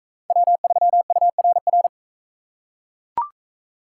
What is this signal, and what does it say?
Morse code, heard as a keyed tone, about 39 wpm, 700 Hz. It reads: W3URR